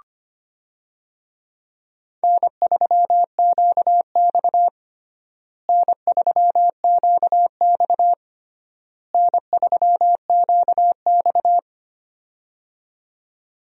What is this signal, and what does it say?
Morse code, heard as a keyed tone, about 25 wpm, 700 Hz. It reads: N3QX N3QX N3QX
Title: USA Call Signs - 4 Characters - Repeated 3x - 25wpm